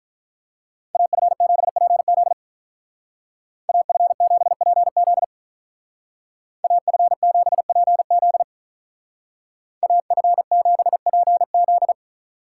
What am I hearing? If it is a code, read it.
Morse code, heard as a keyed tone, about 46 wpm, 700 Hz. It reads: AF7PZ AF7PZ AF7PZ AF7PZ